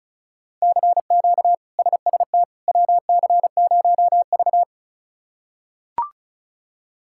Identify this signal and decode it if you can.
Morse code, heard as a keyed tone, about 36 wpm, 700 Hz. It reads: CQ SST WC0V
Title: K1USN SST QSO Practice - 35wpm